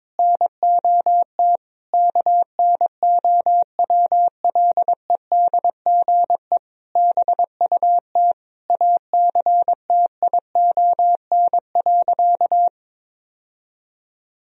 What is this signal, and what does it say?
Morse code, heard as a keyed tone, about 22 wpm, 700 Hz. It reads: NOT KNOWLEDGE BUT ACTION.
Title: Sentences from Top 1000 Words - Review Part 4 - 22wpm